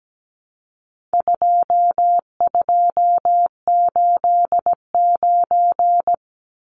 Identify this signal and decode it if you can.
Morse code, heard as a keyed tone, about 17 wpm, 700 Hz. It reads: 2289
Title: Numbers - 4-Digit 17wpm